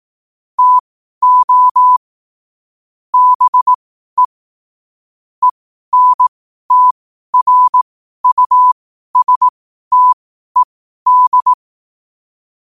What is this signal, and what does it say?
Morse code, heard as a keyed tone, about 18 wpm, spaced as for 12 wpm, 1000 Hz. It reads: TO BE ENTRUSTED
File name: 12wpm